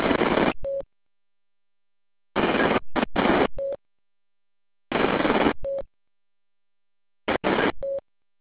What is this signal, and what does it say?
There is no code.